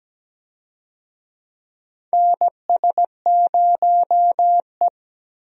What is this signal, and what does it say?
Morse code, heard as a keyed tone, about 17 wpm, 700 Hz. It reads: NS0E
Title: USA Call Signs - 4 Characters - Repeated 2x - 17wpm